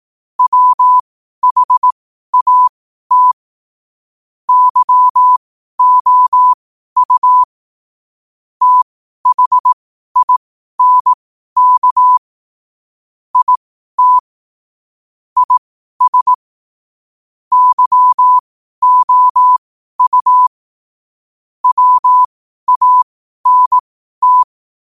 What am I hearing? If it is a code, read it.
Morse code, heard as a keyed tone, about 18 wpm, spaced as for 12 wpm, 1000 Hz. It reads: WHAT YOU THINK IT IS YOU WANT
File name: qotdfiles / 12wpm